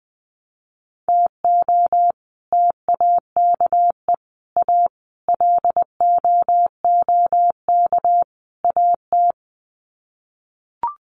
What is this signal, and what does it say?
Morse code, heard as a keyed tone, about 20 wpm, 700 Hz. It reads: TO TAKE A LOOK AT
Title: Sets of 5 Words 20wpm